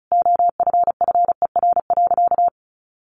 Morse code, heard as a keyed tone, about 34 wpm, 700 Hz. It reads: OFFER.
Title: Sentences from Top 1100 Words 35wpm